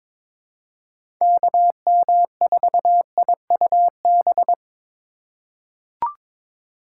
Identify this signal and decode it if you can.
Morse code, heard as a keyed tone, about 22 wpm, 700 Hz. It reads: KM4IUB
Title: USA Call Signs - 6 Characters - 3x Character Spacing - 22wpm